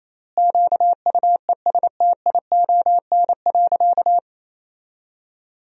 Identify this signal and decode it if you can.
Morse code, heard as a keyed tone, about 28 wpm, 700 Hz. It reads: QUESTION.
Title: 4x Word-Spacing - Sentences from Top 400 Words - 28wpm